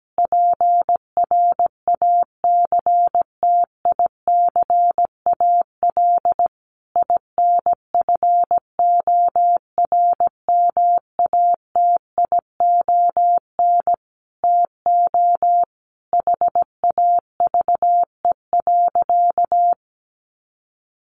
Morse code, heard as a keyed tone, about 17 wpm, 700 Hz. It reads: PRACTICAL INFORMATION TO HAVE.
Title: Sentences from Top 1400 Words 17wpm